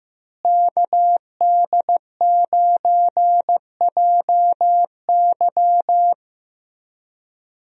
Morse code, heard as a keyed tone, about 16 wpm, 700 Hz. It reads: KD9JY
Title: USA Call Signs - 5 Characters - Repeated 2x - 15wpm